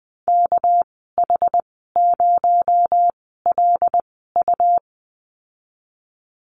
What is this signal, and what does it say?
Morse code, heard as a keyed tone, about 20 wpm, spaced as for 15 wpm, 700 Hz. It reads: KH0LU
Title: 219 Lesson 35g - Number 0 - Call Signs 20wpm